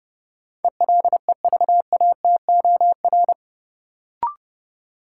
Morse code, heard as a keyed tone, about 30 wpm, 700 Hz. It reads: ELEVATOR
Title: Sentences from Top 1600 Words - New Word Drill Alt - 30wpm